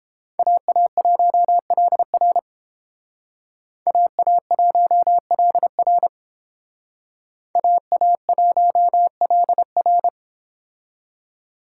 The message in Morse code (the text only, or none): AA1LR AA1LR AA1LR